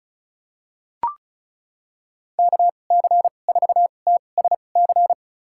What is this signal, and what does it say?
Morse code, heard as a keyed tone, about 36 wpm, spaced as for 25 wpm, 700 Hz. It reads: KC4TSC
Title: USA Call Signs - 6 Characters - 2x Character Spacing - 35wpm